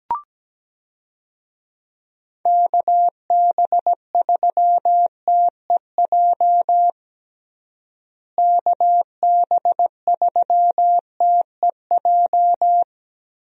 Morse code, heard as a keyed tone, about 17 wpm, 700 Hz. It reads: KB3TEJ KB3TEJ
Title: USA Call Signs - 6 Characters - Repeated 2x - 17wpm